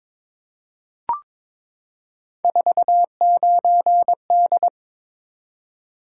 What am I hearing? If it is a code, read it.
Morse code, heard as a keyed tone, about 22 wpm, 700 Hz. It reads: 49D
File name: practice / 22wpm / Three Letter-Number Combinations - 22wpm